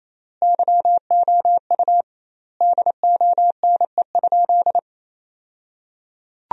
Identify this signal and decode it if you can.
Morse code, heard as a keyed tone, about 28 wpm, 700 Hz. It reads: YOU DONE?
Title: 2x Word-Spacing - Sentences from Top 500 Words - 28wpm